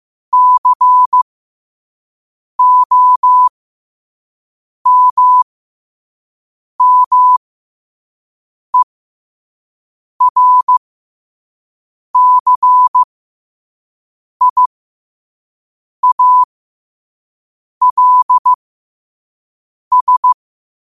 Morse code, heard as a keyed tone, about 15 wpm, spaced as for 5 wpm, 1000 Hz. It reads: COMMERCIALS
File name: qotdfiles / 5wpm